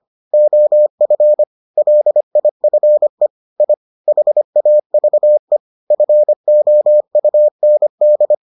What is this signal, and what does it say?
Morse code, heard as a keyed tone, about 25 wpm, 600 Hz. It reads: OF LIFE I HAVE FOUND